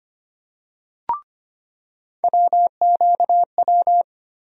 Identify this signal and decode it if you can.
Morse code, heard as a keyed tone, about 25 wpm, 700 Hz. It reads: WQW
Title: USA Call Sign Postfixes - 3 Letters 25wpm